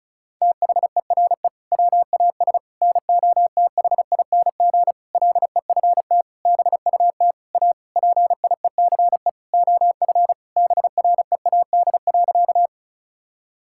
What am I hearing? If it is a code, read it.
Morse code, heard as a keyed tone, about 36 wpm, 700 Hz. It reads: THERE WAS NOTHING LEFT BUT A PIECE OF BREAD.
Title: Sentences from Top 1600 Words 35wpm